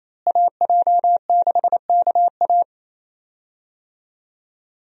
Morse code, heard as a keyed tone, about 28 wpm, 700 Hz. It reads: AJ6KA
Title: USA Call Signs - 5 Characters - Repeated 2x - 28wpm